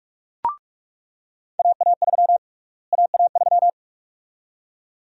Morse code, heard as a keyed tone, about 46 wpm, 700 Hz. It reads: AA3 AA3
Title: USA Call Sign Prefixes - Repeated 2x - 45wpm